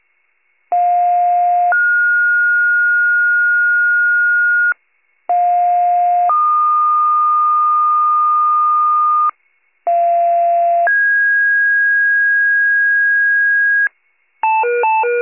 Two-tone page, heard as a one-second tone, then a three-second tone, about 710 then 1430 Hz, about 710 then 1220 Hz, about 690 then 1650 Hz, with noise. It is heard from 0.7 to 13.9 s.